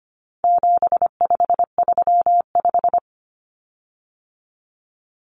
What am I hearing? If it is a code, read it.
Morse code, heard as a keyed tone, about 25 wpm, 700 Hz. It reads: 7535
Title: Common Cut Numbers - 4-Digit 25wpm